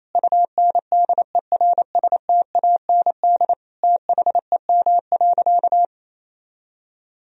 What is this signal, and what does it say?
Morse code, heard as a keyed tone, about 28 wpm, 700 Hz. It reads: UNDERSTAND THEM.
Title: Sentences from Top 1000 Words - Review Part 2 - 28wpm